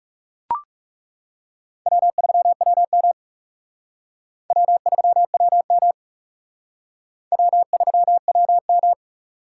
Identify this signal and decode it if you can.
Morse code, heard as a keyed tone, about 40 wpm, 700 Hz. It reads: W3WM W3WM W3WM